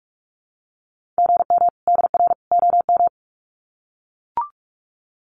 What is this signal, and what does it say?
Morse code, heard as a keyed tone, about 44 wpm, 700 Hz. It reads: GM DR OM
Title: Common QSO Phrases 45wpm